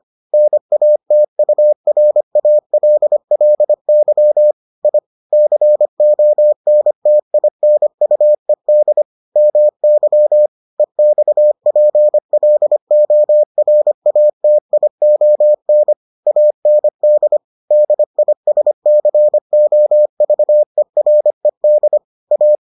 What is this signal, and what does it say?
Morse code, heard as a keyed tone, about 25 wpm, 600 Hz. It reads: NATURALLY I CONTINUED MY EXPLORATION AND DISCOVERED A